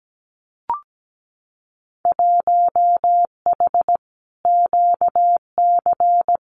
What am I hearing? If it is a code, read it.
Morse code, heard as a keyed tone, about 17 wpm, 700 Hz. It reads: 1H QC
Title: Winter Field Day Exchanges 17wpm